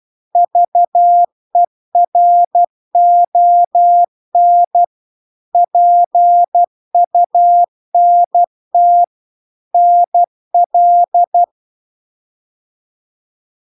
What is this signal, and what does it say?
Morse code, heard as a keyed tone, about 12 wpm, 700 Hz. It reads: VERON PUNT NL
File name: cw-cursus-VERON-a03